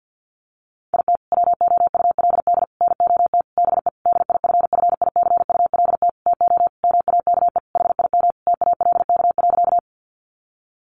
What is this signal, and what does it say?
Morse code, heard as a keyed tone, about 50 wpm, 700 Hz. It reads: IT WOULD NOT BE DIFFICULT TO MAKE HIM TALK.
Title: Sentences from Top 600 Words 50wpm